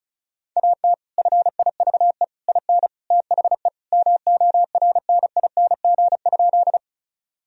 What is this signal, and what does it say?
Morse code, heard as a keyed tone, about 36 wpm, 700 Hz. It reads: AT FIVE IN THE MORNING?
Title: Sentences from Top 1000 Words - Review Part 2 - 35wpm